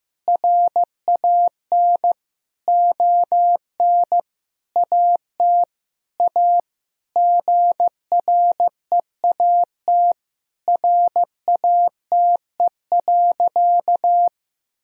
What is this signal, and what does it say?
Morse code, heard as a keyed tone, about 15 wpm, 700 Hz. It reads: RAN ON AT A GREAT RATE.
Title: Sentences from Top 1000 Words - Review Part 4 - 15wpm